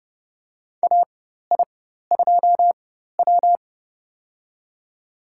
Morse code, heard as a keyed tone, about 29 wpm, spaced as for 14 wpm, 700 Hz. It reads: AI2W
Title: USA Call Signs - 4 Characters - 4x Character Spacing - 30wpm